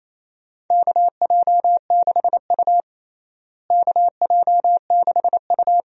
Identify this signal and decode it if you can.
Morse code, heard as a keyed tone, about 28 wpm, 700 Hz. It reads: KJ6U KJ6U
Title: USA Call Signs - 4 Characters - Repeated 2x - 28wpm